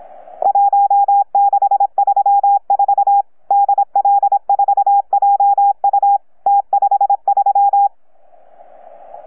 Morse code, heard as a keyed tone, about 26 wpm, 810 Hz. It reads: 1634 DL4JU T53
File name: UP2L-20-CW